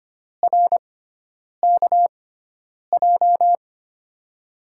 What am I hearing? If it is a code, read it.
Morse code, heard as a keyed tone, about 25 wpm, spaced as for 9 wpm, 700 Hz. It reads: RKJ
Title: USA Call Signs - 5 Characters - 6x Character Spacing - 25wpm